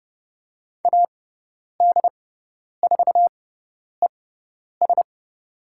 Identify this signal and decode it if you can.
Morse code, heard as a keyed tone, about 30 wpm, spaced as for 10 wpm, 700 Hz. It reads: AD4ES
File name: Lesson 17g - Letter u - Call Signs 30wpm